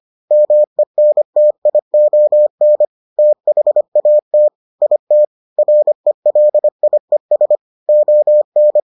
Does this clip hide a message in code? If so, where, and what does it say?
Yes: Morse code from 0.3 to 8.8 s, heard as a keyed tone, about 25 wpm, 600 Hz. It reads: MENTION THAT IT RELIES ON